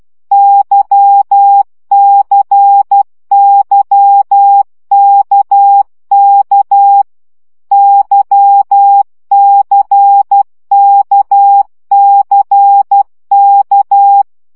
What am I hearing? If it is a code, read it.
Morse code, heard as a keyed tone, about 12 wpm, 800 Hz. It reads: YCYKK YCKCK
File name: CW gyakorlas k c y 10 baud